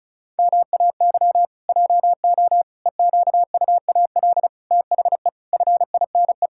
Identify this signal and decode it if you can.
Morse code, heard as a keyed tone, about 36 wpm, 700 Hz. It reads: MAY JO EQUAL THE FINE
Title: Pangram Challenge v2 - 35wpm